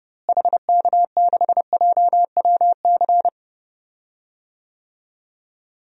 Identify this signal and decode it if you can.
Morse code, heard as a keyed tone, about 32 wpm, 700 Hz. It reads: HK6JWC
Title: DXCC-100 Call Signs - 6 Characters - Repeated 3x - 30wpm